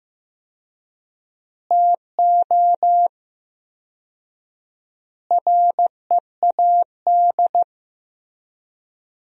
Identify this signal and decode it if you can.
Morse code, heard as a keyed tone, about 15 wpm, 700 Hz. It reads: TO READ
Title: 4x Word-Spacing - Sentences from Top 500 Words - 15wpm